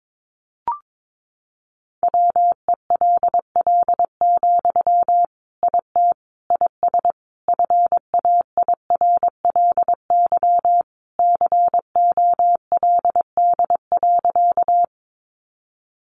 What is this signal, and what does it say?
Morse code, heard as a keyed tone, about 22 wpm, 700 Hz. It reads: WELL, IT IS FAIRLY COLD.
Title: Sentences from Top 1100 Words 22wpm